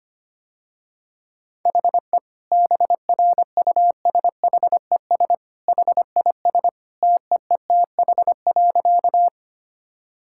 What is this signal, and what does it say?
Morse code, heard as a keyed tone, about 25 wpm, 700 Hz. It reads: HE BRUSHES HIS TEETH.
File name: Easy Sentences with Hints - 25wpm